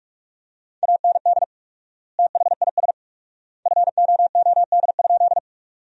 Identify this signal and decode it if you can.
Morse code, heard as a keyed tone, about 46 wpm, 700 Hz. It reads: AND THIS FOOD?
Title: 4x Word-Spacing - Sentences from Top 300 Words - 45wpm